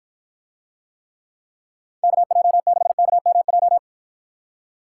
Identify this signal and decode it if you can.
Morse code, heard as a keyed tone, about 52 wpm, 700 Hz. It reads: KJ6KMJ